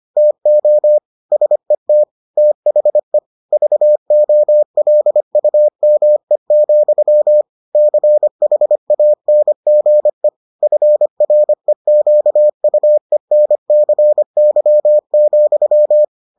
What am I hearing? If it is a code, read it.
Morse code, heard as a keyed tone, about 25 wpm, 600 Hz. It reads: TO SET THE VOLUME, CHANGE FREQUENCY,